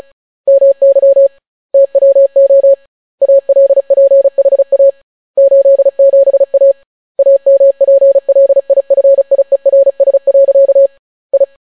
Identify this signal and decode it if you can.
Morse code, heard as a keyed tone, about 35 wpm, 550 Hz. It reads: MY TWO ALPHA 87A AMPLIFIERS. I